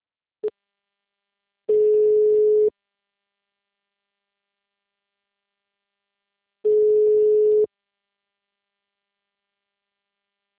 Ringback tone (Europe, 425 Hz).
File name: out- -2030